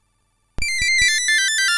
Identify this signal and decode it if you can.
There is no code.